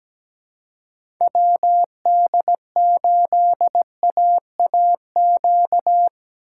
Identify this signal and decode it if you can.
Morse code, heard as a keyed tone, about 17 wpm, 700 Hz. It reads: WD8AAQ
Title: USA Call Signs - 6 Characters - 4x Character Spacing - 17wpm